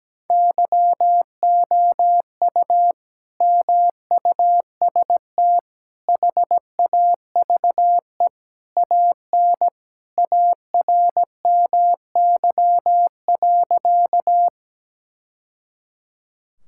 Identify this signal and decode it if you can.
Morse code, heard as a keyed tone, about 17 wpm, 700 Hz. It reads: YOU MUST HAVE AN ARMY.
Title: Sentences from Top 1000 Words - Review Part 5 - 17wpm